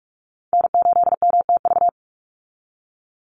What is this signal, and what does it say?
Morse code, heard as a keyed tone, about 44 wpm, 700 Hz. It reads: N8MTV
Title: Call Signs Repeated 3x - 45wpm